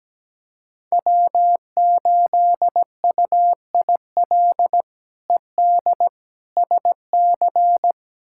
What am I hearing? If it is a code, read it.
Morse code, heard as a keyed tone, about 17 wpm, 700 Hz. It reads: W8UIL ED SC
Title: K1USN SST QSO Practice - 17wpm